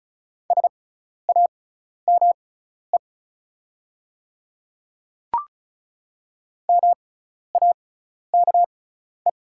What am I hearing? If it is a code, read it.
Morse code, heard as a keyed tone, about 36 wpm, spaced as for 12 wpm, 690 Hz. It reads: SAME MAKE
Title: 6x Character Spacing - Top 200 Words - 35wpm